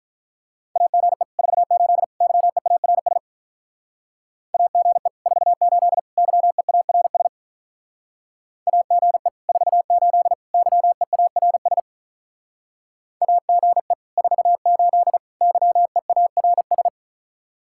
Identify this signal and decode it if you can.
Morse code, heard as a keyed tone, about 43 wpm, 700 Hz. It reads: AGE 48 YEARS AGE 48 YEARS AGE 48 YEARS AGE 48 YEARS
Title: Speed-Racing Common QSO Phrases - 35wpm